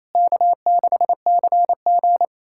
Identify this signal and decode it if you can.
Morse code, heard as a keyed tone, about 28 wpm, 700 Hz. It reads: K6CG
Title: USA Callsign DE Callsign - 4 Characters - 28wpm